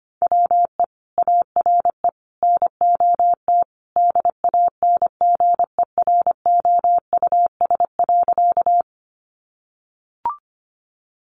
Morse code, heard as a keyed tone, about 25 wpm, 700 Hz. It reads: WE ARE NOT DANGEROUS.
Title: Sentences from Top 1200 Words 25wpm